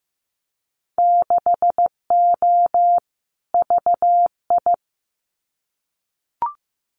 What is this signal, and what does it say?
Morse code, heard as a keyed tone, about 15 wpm, 700 Hz. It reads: 6O VI
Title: Winter Field Day Exchanges Encore 15wpm